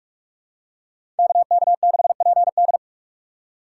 Morse code, heard as a keyed tone, about 46 wpm, 700 Hz. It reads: KK6PD
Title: USA Call Signs - 5 Characters - 3x Character Spacing - 45wpm